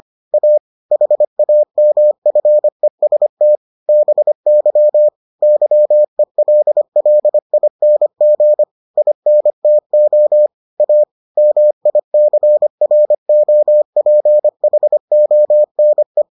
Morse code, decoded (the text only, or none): A HAMFEST BY YELLING INTO A MICROPHONE